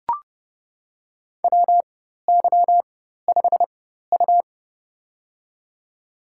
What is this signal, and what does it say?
Morse code, heard as a keyed tone, about 30 wpm, spaced as for 14 wpm, 700 Hz. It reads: WY5U
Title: USA Call Signs - 4x Character Spacing - 30wpm